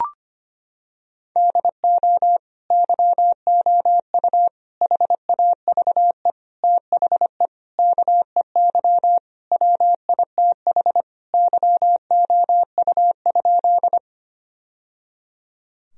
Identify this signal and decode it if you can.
Morse code, heard as a keyed tone, about 25 wpm, 700 Hz. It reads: DO YOU HAVE THE KEY WITH YOU?
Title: Sentences from Top 1000 Words - Review Part 4 - 25wpm